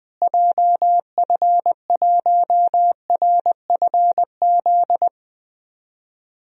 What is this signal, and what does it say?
Morse code, heard as a keyed tone, about 20 wpm, 700 Hz. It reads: JF1RFZ
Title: DXCC-100 Call Signs - 6 Characters - Repeated 2x - 20wpm